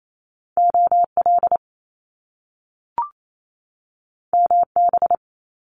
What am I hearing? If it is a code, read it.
Morse code, heard as a keyed tone, about 28 wpm, 700 Hz. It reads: OL MB
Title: Top Two Letter Combinations in Words 28wpm